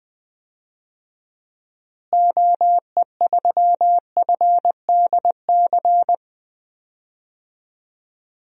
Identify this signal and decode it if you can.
Morse code, heard as a keyed tone, about 21 wpm, 700 Hz. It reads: OE3FDC
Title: DXCC-100 Call Signs - 6 Characters - Repeated 2x - 20wpm